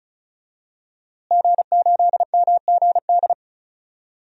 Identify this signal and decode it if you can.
Morse code, heard as a keyed tone, about 34 wpm, 700 Hz. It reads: G8MGD
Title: DXCC-100 Call Signs - 5 Characters - Repeated 2x - 35wpm